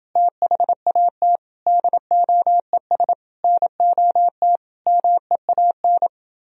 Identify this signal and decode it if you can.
Morse code, heard as a keyed tone, about 28 wpm, 700 Hz. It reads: THAT DOES NOT MEAN